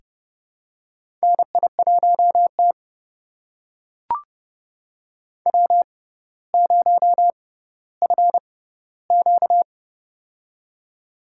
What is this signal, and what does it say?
Morse code, heard as a keyed tone, about 32 wpm, spaced as for 16 wpm, 700 Hz. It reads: NI1T W0FQ
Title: USA Call Signs - 4 Characters - 6x Character Spacing - 30wpm